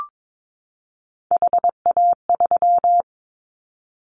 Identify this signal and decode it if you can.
Morse code, heard as a keyed tone, about 22 wpm, 700 Hz. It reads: HA3
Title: Callsign Prefixes for Top 50 DXCC - 22wpm